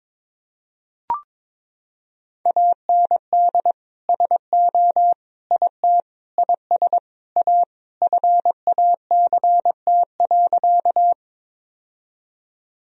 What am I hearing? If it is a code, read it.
Morse code, heard as a keyed tone, about 22 wpm, 700 Hz. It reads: AND SO IT IS A FACT.
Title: Sentences from Top 1000 Words - Review Part 3 - 22wpm